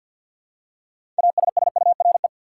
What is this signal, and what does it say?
Morse code, heard as a keyed tone, about 47 wpm, 700 Hz. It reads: ASSURE
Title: Six Letter Words - 50wpm